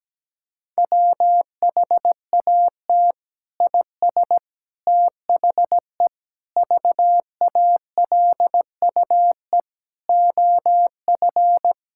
Morse code, heard as a keyed tone, about 17 wpm, 700 Hz. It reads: WHAT IS THE VALUE OF
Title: Sentences from Top 1000 Words - Review Part 6 - 17wpm